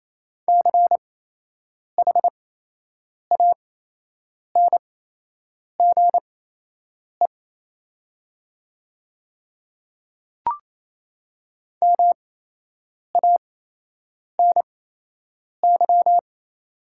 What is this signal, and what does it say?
Morse code, heard as a keyed tone, about 28 wpm, spaced as for 8 wpm, 690 Hz. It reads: CHANGE MANY
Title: 8x Character Spacing - Top 200 Words - 28wpm